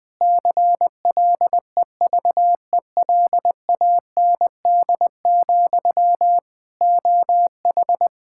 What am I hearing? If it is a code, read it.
Morse code, heard as a keyed tone, about 20 wpm, 700 Hz. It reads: CLEVELAND, OH
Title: K1USN SST QSO Practice 20wpm